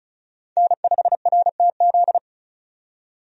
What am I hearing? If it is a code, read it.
Morse code, heard as a keyed tone, about 34 wpm, 700 Hz. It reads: N5RTZ
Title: USA Call Signs - 5 Characters - 35wpm